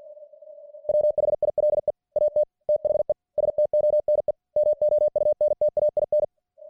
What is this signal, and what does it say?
Morse code, heard as a keyed tone, about 50 wpm, 600 Hz. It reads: WHILE AT THE STONE MOUNTAIN